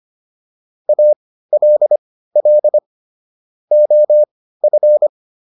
Morse code, heard as a keyed tone, about 25 wpm, spaced as for 15 wpm, 600 Hz. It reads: ALL OF